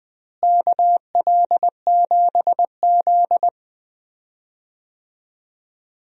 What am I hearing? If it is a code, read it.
Morse code, heard as a keyed tone, about 21 wpm, 700 Hz. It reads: KL7Z
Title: DXCC-100 Call Signs - 4 Characters - Repeated 2x - 20wpm